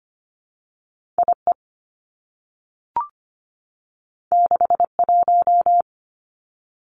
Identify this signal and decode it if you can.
Morse code, heard as a keyed tone, about 25 wpm, 700 Hz. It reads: IE 61
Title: Two Challenging Letter-Number Combinations 25wpm